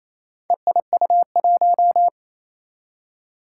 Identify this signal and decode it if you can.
Morse code, heard as a keyed tone, about 27 wpm, 700 Hz. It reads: EIU1
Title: Four Challenging Letter-Number Combinations 28wpm